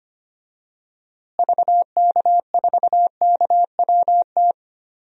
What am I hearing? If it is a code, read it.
Morse code, heard as a keyed tone, about 25 wpm, 700 Hz. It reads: VK4KWT
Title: DXCC-100 Call Signs - 6 Characters - Repeated 2x - 25wpm